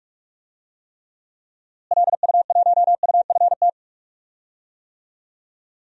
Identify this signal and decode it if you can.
Morse code, heard as a keyed tone, about 46 wpm, 700 Hz. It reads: LU1UFT
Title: DXCC-100 Call Signs - 6 Characters - Repeated 2x - 45wpm